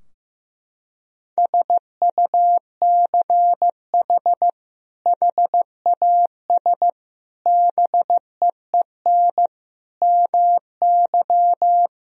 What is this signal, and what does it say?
Morse code, heard as a keyed tone, about 16 wpm, 700 Hz. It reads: SUCH HAS BEEN MY
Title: Sentences from Top 1000 Words - Review Part 3 - 15wpm